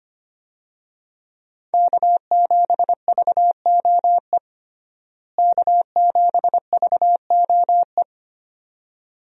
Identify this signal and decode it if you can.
Morse code, heard as a keyed tone, about 25 wpm, 700 Hz. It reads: K7VOE K7VOE
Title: USA Call Signs - 5 Characters - Repeated 2x - 25wpm